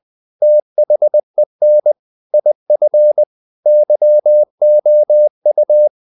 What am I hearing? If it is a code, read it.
Morse code, heard as a keyed tone, about 20 wpm, 600 Hz. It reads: THEN IF YOU
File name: the old man and the sea / the old man and the sea part 1 20wpm